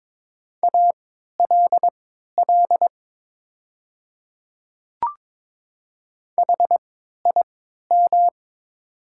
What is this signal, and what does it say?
Morse code, heard as a keyed tone, about 22 wpm, spaced as for 12 wpm, 700 Hz. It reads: ALL HIM
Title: 3x Character Spacing - Top 100 Words - 22wpm